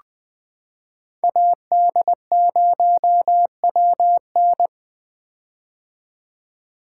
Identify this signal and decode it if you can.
Morse code, heard as a keyed tone, about 20 wpm, 700 Hz. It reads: AD0WN
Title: 2023 POTA Hunter Call Signs - 20wpm